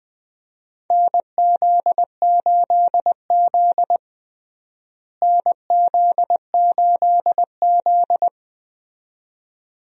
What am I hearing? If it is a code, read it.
Morse code, heard as a keyed tone, about 20 wpm, 700 Hz. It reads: NZ8Z NZ8Z
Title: USA Call Signs - 4 Characters - Repeated 2x - 20wpm